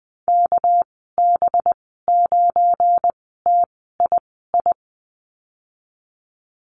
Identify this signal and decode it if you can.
Morse code, heard as a keyed tone, about 19 wpm, spaced as for 14 wpm, 700 Hz. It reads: KB9TII
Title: 238 Lesson 38g - Letter q - Call Signs 20wpm